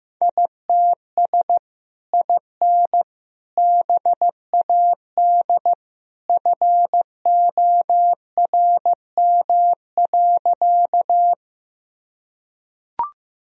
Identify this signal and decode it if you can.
Morse code, heard as a keyed tone, about 15 wpm, 700 Hz. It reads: ITS IN BAD FORM.
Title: Sentences from Top 1000 Words - Review Part 1 - 15wpm